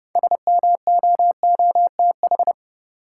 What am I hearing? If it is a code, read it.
Morse code, heard as a keyed tone, about 30 wpm, 700 Hz. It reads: SMOOTH
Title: Sentences from Top 1400 Words - New Word Drill Alt - 30wpm